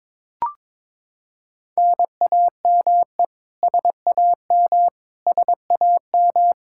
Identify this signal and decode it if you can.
Morse code, heard as a keyed tone, about 22 wpm, 700 Hz. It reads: NAME SAM SAM
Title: Practice QSO - Exchange 1 - Beginning - 22wpm